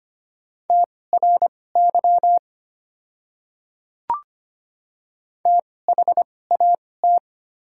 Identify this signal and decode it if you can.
Morse code, heard as a keyed tone, about 25 wpm, spaced as for 18 wpm, 700 Hz. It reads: TRY THAT
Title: 2x Character Spacing - Top 200 Words - 25wpm